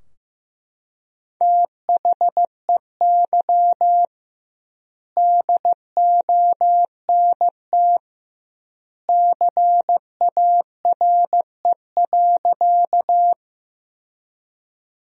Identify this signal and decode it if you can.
Morse code, heard as a keyed tone, about 15 wpm, 700 Hz. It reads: THEY DONT CARE.
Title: 2x Word-Spacing - Sentences from Top 300 Words - 15wpm